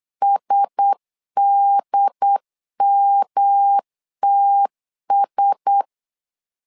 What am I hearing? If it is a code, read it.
Morse code, heard as a keyed tone, about 8 wpm, 800 Hz. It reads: SDMTS
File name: LEZIONI / 4 Q1S 8wpm 2710